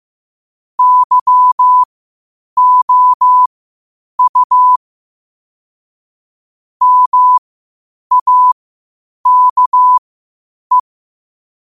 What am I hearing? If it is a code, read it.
Morse code, heard as a keyed tone, about 15 wpm, spaced as for 8 wpm, 1000 Hz. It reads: YOU MAKE
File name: qotdfiles / 8wpm